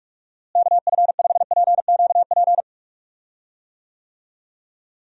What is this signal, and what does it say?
Morse code, heard as a keyed tone, about 46 wpm, 700 Hz. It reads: KF5PQP
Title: USA Call Signs - 6 Characters - Repeated 2x - 45wpm